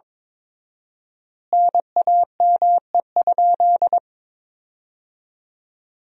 Morse code, heard as a keyed tone, about 22 wpm, 700 Hz. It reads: NAME?
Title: 4x Word-Spacing - Sentences from Top 300 Words - 22wpm